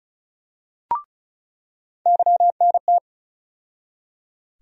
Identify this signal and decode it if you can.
Morse code, heard as a keyed tone, about 36 wpm, 700 Hz. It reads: YNT
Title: USA Call Sign Postfixes - 3 Letters 35wpm